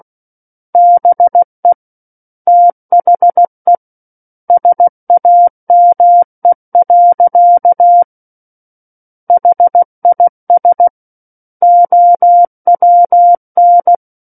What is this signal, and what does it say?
Morse code, heard as a keyed tone, about 16 wpm, 700 Hz. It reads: BE THE SAME. HIS OWN
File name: Book One / 07-Chapter 7 - 16 WPM